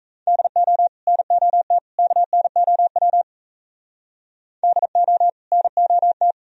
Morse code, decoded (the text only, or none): DO NOT KNOW DO NOT